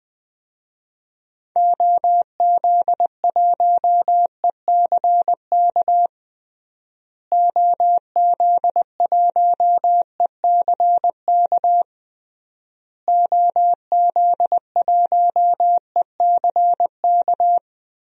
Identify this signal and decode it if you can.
Morse code, heard as a keyed tone, about 21 wpm, 700 Hz. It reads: OZ1ECK OZ1ECK OZ1ECK
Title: DXCC-100 Call Signs - 6 Characters - Repeated 3x - 20wpm